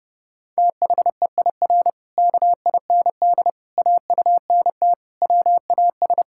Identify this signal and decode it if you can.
Morse code, heard as a keyed tone, about 30 wpm, 700 Hz. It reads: THEIR KIND AUNT WAS
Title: Pangram Challenge v2 - 30wpm